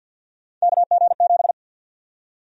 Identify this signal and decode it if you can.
Morse code, heard as a keyed tone, about 52 wpm, 700 Hz. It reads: KG7
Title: USA Call Sign Prefixes - Repeated 2x - 50wpm